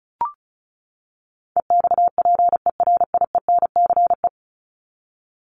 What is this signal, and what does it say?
Morse code, heard as a keyed tone, about 34 wpm, 700 Hz. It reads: EXPERIENCE
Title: Top 501-750 Words 35wpm